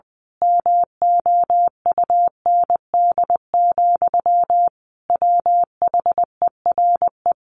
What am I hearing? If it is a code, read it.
Morse code, heard as a keyed tone, about 20 wpm, 700 Hz. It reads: MOUND, WHERE
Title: A Sound Of Thunder 20wpm